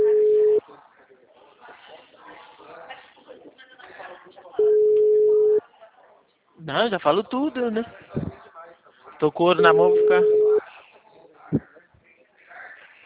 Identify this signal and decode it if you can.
Ringback tone (Europe, 425 Hz), with speech.